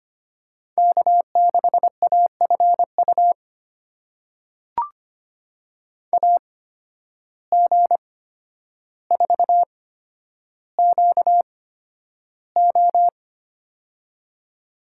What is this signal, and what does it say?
Morse code, heard as a keyed tone, about 25 wpm, spaced as for 11 wpm, 700 Hz. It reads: K6AFU AG4QO
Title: USA Call Signs - 5 Characters - 8x Character Spacing - 25wpm